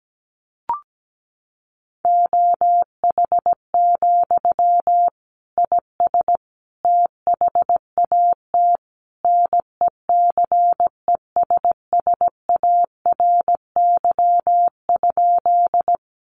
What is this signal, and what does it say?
Morse code, heard as a keyed tone, about 17 wpm, 700 Hz. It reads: OH, IS THAT NECESSARY?
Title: Sentences from Top 800 Words 17wpm